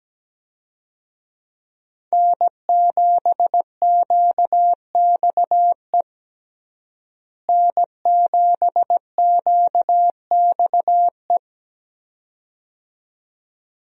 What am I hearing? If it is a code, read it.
Morse code, heard as a keyed tone, about 17 wpm, 700 Hz. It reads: N7QXE N7QXE
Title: USA Call Signs - 5 Characters - Repeated 2x - 17wpm